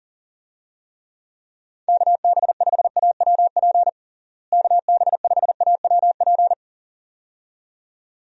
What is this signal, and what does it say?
Morse code, heard as a keyed tone, about 40 wpm, 700 Hz. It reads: KB5AWP KB5AWP
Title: USA Call Signs - 6 Characters - Repeated 2x - 40wpm